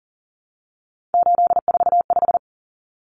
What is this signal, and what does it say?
Morse code, heard as a keyed tone, about 40 wpm, 700 Hz. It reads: 845
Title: Common Cut Numbers - 3-Digit 40wpm